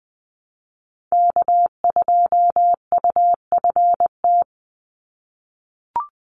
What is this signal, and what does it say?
Morse code, heard as a keyed tone, about 20 wpm, 700 Hz. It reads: K2UFT
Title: Call Signs - 20wpm